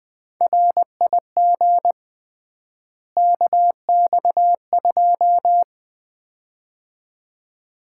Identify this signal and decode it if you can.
Morse code, heard as a keyed tone, about 20 wpm, 700 Hz. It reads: RIG KX2
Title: Common QSO Phrases - 3x Word-Spacing - 20wpm